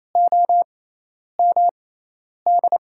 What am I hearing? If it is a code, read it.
Morse code, heard as a keyed tone, about 28 wpm, spaced as for 10 wpm, 700 Hz. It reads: OMD
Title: USA Call Signs - 6 Characters - 6x Character Spacing - 28wpm